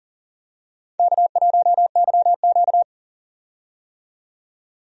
Morse code, heard as a keyed tone, about 40 wpm, 700 Hz. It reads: K1YQ
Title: USA Call Signs - 4 Characters - 40wpm